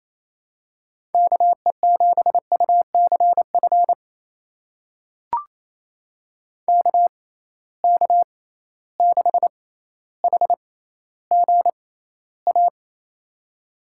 Morse code, heard as a keyed tone, about 28 wpm, spaced as for 14 wpm, 700 Hz. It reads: KE7UCF KK6HGA
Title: USA Call Signs - 6 Characters - 6x Character Spacing - 28wpm